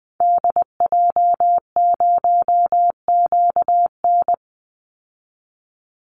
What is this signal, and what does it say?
Morse code, heard as a keyed tone, about 20 wpm, 700 Hz. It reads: DJ0QN
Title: Call Signs Repeated 3x - 20wpm